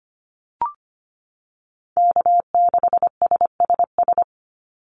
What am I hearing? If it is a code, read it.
Morse code, heard as a keyed tone, about 25 wpm, 700 Hz. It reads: K6SSS
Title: Call Signs - 25wpm